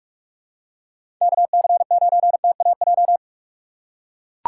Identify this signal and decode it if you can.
Morse code, heard as a keyed tone, about 45 wpm, 700 Hz. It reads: KC9TAJ
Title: USA Call Signs - 6 Characters - 2x Character Spacing - 45wpm